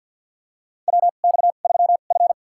Morse code, heard as a keyed tone, about 49 wpm, spaced as for 36 wpm, 690 Hz. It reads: WX3F
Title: USA Call Signs - 2x Character Spacing - 50wpm